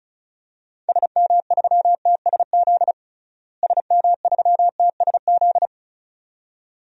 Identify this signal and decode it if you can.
Morse code, heard as a keyed tone, about 36 wpm, 700 Hz. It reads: SM3TSZ SM3TSZ
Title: DXCC-100 Call Signs - 6 Characters - Repeated 2x - 35wpm